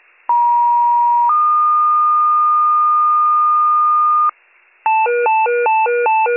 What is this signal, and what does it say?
Two-tone page, heard as a one-second tone, then a three-second tone, about 950 then 1220 Hz, with noise. It is heard from 0.3 to 4.3 s.